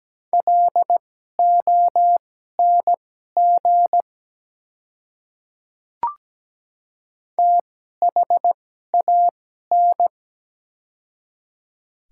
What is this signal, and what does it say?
Morse code, heard as a keyed tone, about 17 wpm, spaced as for 12 wpm, 700 Hz. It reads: LONG THAN
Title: 2x Character Spacing - Top 100 Words - 17wpm